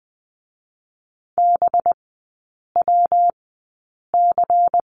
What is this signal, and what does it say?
Morse code, heard as a keyed tone, about 20 wpm, spaced as for 8 wpm, 700 Hz. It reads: BWC
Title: 184 Lesson 30e - Letter b - Three Characters 20wpm